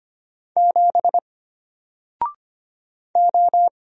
Morse code, heard as a keyed tone, about 25 wpm, 700 Hz. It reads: 7 O
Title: Single Letter-Number Alt 25wpm